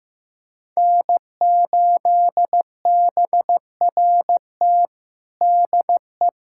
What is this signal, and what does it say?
Morse code, heard as a keyed tone, about 15 wpm, 700 Hz. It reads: N8BRT DE
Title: USA Callsign DE Callsign - 5 Characters - 15wpm